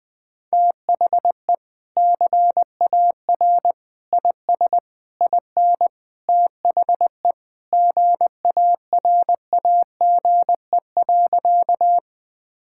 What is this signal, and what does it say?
Morse code, heard as a keyed tone, about 21 wpm, 700 Hz. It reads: THE CAR IS IN THE GARAGE.
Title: Sentences from Top 1600 Words 20wpm